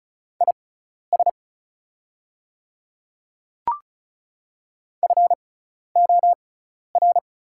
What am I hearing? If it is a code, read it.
Morse code, heard as a keyed tone, about 35 wpm, spaced as for 12 wpm, 690 Hz. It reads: IS FOR
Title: 6x Character Spacing - Top 200 Words - 35wpm